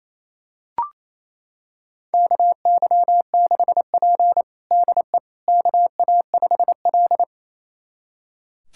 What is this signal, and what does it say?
Morse code, heard as a keyed tone, about 28 wpm, 700 Hz. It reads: KY6P DE KA5L
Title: USA Callsign DE Callsign - 4 Characters - 28wpm